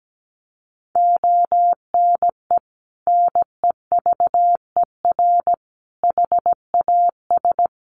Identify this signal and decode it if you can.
Morse code, heard as a keyed tone, about 17 wpm, 700 Hz. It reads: ONE NEVER HAS
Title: Sentences from Top 300 Words 17wpm